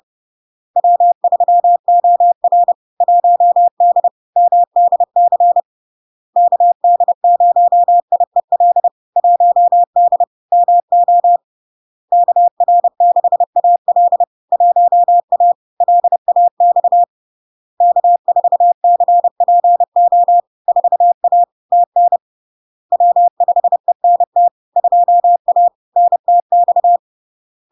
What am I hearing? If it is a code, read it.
Morse code, heard as a keyed tone, about 30 wpm, 700 Hz. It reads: W3OR 1D MDC KD0IEL 1D MO KR6AL 1A LAX K4CPO 4A TN W5ENT 2A NTX
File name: LARK-FD-2022-Log-in-CW-30wpm-700-Hz